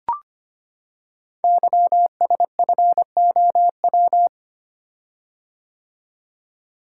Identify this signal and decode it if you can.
Morse code, heard as a keyed tone, about 25 wpm, 700 Hz. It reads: YSFOW
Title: Code Groups - 5 Letter:Numbers - 25wpm